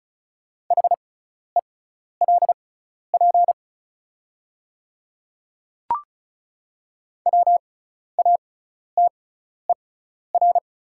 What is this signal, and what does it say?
Morse code, heard as a keyed tone, about 34 wpm, spaced as for 12 wpm, 690 Hz. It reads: HELP WATER
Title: 6x Character Spacing - Top 200 Words - 35wpm